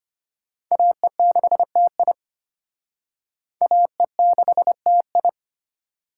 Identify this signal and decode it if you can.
Morse code, heard as a keyed tone, about 31 wpm, 700 Hz. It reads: AE6TI AE6TI
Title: Speed Racing - USA 2x2 Call Signs 25wpm